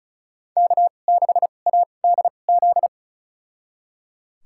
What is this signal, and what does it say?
Morse code, heard as a keyed tone, about 36 wpm, spaced as for 25 wpm, 700 Hz. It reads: K6ADZ
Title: USA Call Signs - 5 Characters - 2x Character Spacing - 35wpm